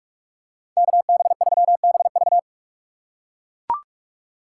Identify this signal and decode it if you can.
Morse code, heard as a keyed tone, about 43 wpm, 700 Hz. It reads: KB3BV
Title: USA Call Signs - 5 Characters - 4x Character Spacing - 45wpm